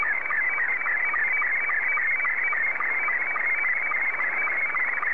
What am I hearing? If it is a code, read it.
SSTV (slow-scan television), Scottie 2.